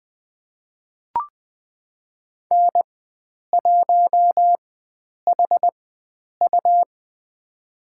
Morse code, heard as a keyed tone, about 20 wpm, spaced as for 9 wpm, 700 Hz. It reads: N1HU
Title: USA Call Signs - 4 Characters - 4x Character Spacing - 20wpm